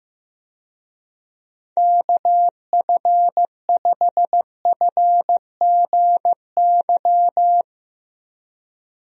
Morse code, heard as a keyed tone, about 16 wpm, 700 Hz. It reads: KF5FGY
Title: USA Call Signs - 6 Characters - Repeated 2x - 15wpm